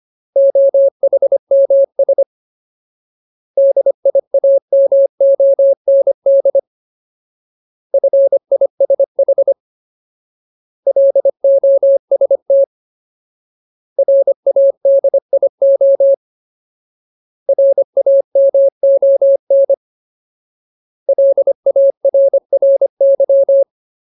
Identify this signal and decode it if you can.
Morse code, heard as a keyed tone, about 25 wpm, 550 Hz. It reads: OHMS DIAMOND FISH LOST RADIO RAMON LARRY
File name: Ultimate Word List / 25WPM Ultimate